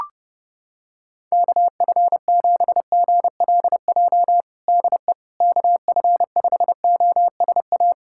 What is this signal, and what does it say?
Morse code, heard as a keyed tone, about 31 wpm, 700 Hz. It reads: KF7GLJ DE KF5OSA
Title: USA Callsign DE Callsign - 6 Characters - 30wpm